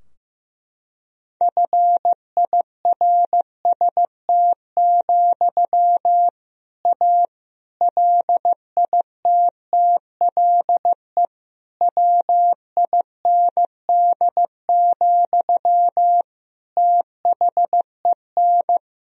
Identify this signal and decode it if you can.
Morse code, heard as a keyed tone, about 16 wpm, 700 Hz. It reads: FIRST, A LITTLE WIND, THEN
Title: Sentences from Top 1000 Words - Review Part 2 - 15wpm